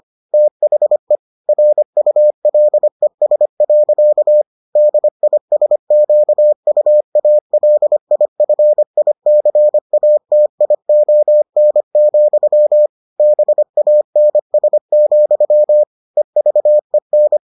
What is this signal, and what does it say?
Morse code, heard as a keyed tone, about 25 wpm, 600 Hz. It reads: THE RULES. DISQUALIFICATION, BANS, EVEN